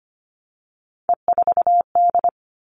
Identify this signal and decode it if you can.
Morse code, heard as a keyed tone, about 25 wpm, 700 Hz. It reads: E4D
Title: All Cut Numbers - 3-Digit 25wpm